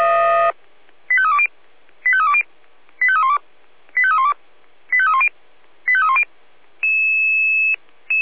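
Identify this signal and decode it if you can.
Five-tone selective call (ZVEI), heard as quick runs of five tones, tones of about 70 ms. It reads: 85320 85320 85312 85312 85310 85310
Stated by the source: Five-tone ZVEI selective call received on an SDR.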